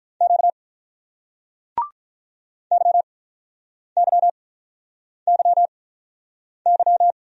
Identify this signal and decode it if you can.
Morse code, heard as a keyed tone, about 47 wpm, spaced as for 9 wpm, 700 Hz. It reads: Q YYYY